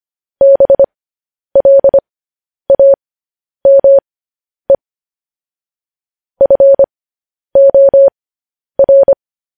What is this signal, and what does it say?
Morse code, heard as a keyed tone, about 25 wpm, spaced as for 10 wpm, 550 Hz. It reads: BLAME FOR
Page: News Headlines in Morse Code at 10 WPM